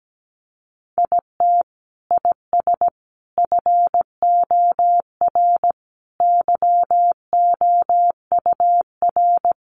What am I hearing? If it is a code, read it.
Morse code, heard as a keyed tone, about 17 wpm, 700 Hz. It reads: IT IS FOR YOUR
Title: Sentences from Top 700 Words 17wpm